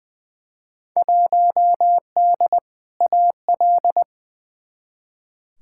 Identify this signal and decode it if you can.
Morse code, heard as a keyed tone, about 21 wpm, 700 Hz. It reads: 1D AL
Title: Field Day Exchanges - 20wpm